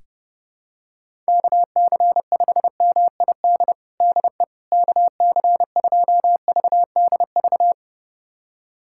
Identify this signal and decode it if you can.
Morse code, heard as a keyed tone, about 30 wpm, 700 Hz. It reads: KC5MID DE KC2VDV
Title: USA Callsign DE Callsign - 6 Characters - 30wpm